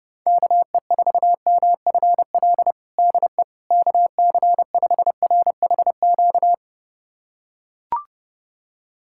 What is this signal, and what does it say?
Morse code, heard as a keyed tone, about 30 wpm, 700 Hz. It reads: KE4MFL DE KC5RHQ
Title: USA Callsign DE Callsign 30wpm